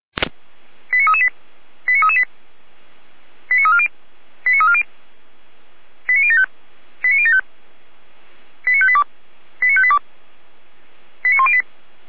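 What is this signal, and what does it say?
Five-tone selective call (ZVEI), heard as quick runs of five tones, tones of about 70 ms. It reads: 89338 89338 89340 89340 89075 89075 89582 89582 89108
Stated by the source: ZVEI five-tone call received on SDR.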